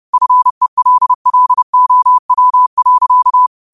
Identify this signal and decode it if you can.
Morse code, heard as a keyed tone, about 30 wpm, 1000 Hz. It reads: FELLOW.